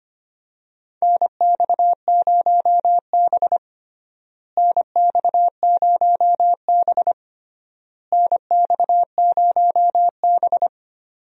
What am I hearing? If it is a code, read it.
Morse code, heard as a keyed tone, about 25 wpm, 700 Hz. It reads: NX0B NX0B NX0B
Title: USA Call Signs - 4 Characters - Repeated 3x - 25wpm